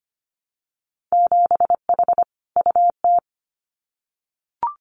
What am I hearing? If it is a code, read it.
Morse code, heard as a keyed tone, about 25 wpm, 700 Hz. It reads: 7H UT
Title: Winter Field Day Exchanges Encore 25wpm